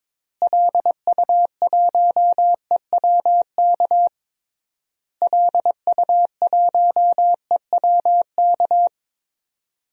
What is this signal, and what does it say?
Morse code, heard as a keyed tone, about 22 wpm, 700 Hz. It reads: LU1EWK LU1EWK
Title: DXCC-100 Call Signs - 6 Characters - Repeated 3x - 22wpm